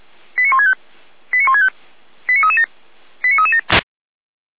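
Five-tone selective call (ZVEI), heard as quick runs of five tones, tones of about 70 ms. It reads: 89165 89165 89337 89337
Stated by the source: ZVEI five-tone selective call received on an SDR, served by feuerwehr-orlamuende (Orlamünde fire brigade).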